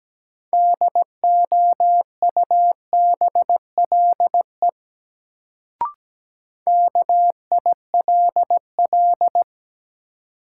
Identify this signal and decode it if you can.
Morse code, heard as a keyed tone, about 17 wpm, 700 Hz. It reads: DOUBLE KILL
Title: Sentences from Top 1400 Words - New Word Drill - 17wpm